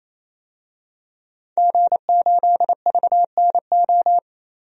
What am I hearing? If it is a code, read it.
Morse code, heard as a keyed tone, about 28 wpm, 700 Hz. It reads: G8VNO
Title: DXCC-100 Call Signs - 5 Characters - 28wpm